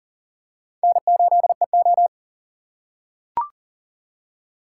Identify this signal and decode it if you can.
Morse code, heard as a keyed tone, about 38 wpm, 700 Hz. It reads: N8EO
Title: USA Call Signs - 4 Characters - 6x Character Spacing - 40wpm